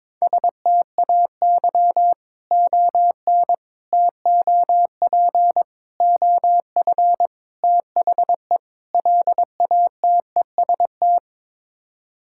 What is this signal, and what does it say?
Morse code, heard as a keyed tone, about 22 wpm, 700 Hz. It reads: STAY ON TOP OF THE LATEST
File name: practice / 22wpm / Sets of 6 Words - 22wpm